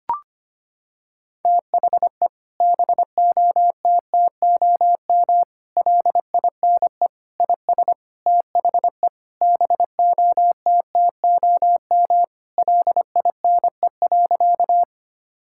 Morse code, heard as a keyed tone, about 25 wpm, 700 Hz. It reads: THE BOTTOM LINE IS THE BOTTOM LINE.
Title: Sentences from Top 1000 Words - Review Part 6 - 25wpm